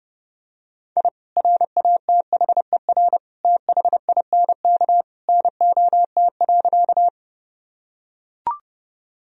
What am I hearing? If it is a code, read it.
Morse code, heard as a keyed tone, about 30 wpm, 700 Hz. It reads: I RATHER THINK NOT.
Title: Sentences from Top 1000 Words - Review Part 4 - 30wpm